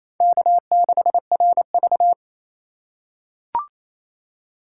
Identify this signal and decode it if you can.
Morse code, heard as a keyed tone, about 28 wpm, 700 Hz. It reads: K6RV
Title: USA Call Signs - 4 Characters - 3x Character Spacing - 28wpm